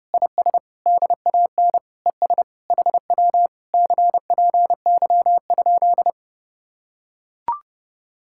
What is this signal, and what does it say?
Morse code, heard as a keyed tone, about 30 wpm, 700 Hz. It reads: IS DAN ES HW CPY?